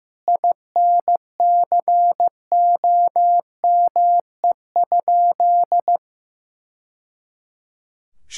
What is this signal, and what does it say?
Morse code, heard as a keyed tone, about 15 wpm, 700 Hz. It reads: INCOME?
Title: Sentences from Top 1000 Words - Review Part 5 - 15wpm